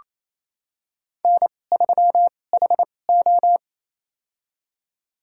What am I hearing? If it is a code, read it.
Morse code, heard as a keyed tone, about 28 wpm, spaced as for 20 wpm, 700 Hz. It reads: N3HO
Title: USA Call Signs - 4 Characters - 2x Character Spacing - 28wpm